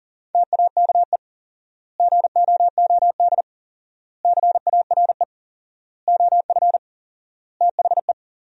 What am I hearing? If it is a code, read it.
Morse code, heard as a keyed tone, about 40 wpm, 700 Hz. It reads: TAKE GOOD CARE OF THE
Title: 4x Word-Spacing - Sentences from Top 400 Words - 40wpm